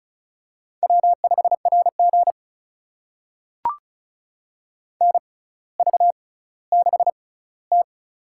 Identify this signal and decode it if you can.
Morse code, heard as a keyed tone, about 36 wpm, spaced as for 18 wpm, 700 Hz. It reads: W5RG NV6T
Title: USA Call Signs - 4 Characters - 6x Character Spacing - 35wpm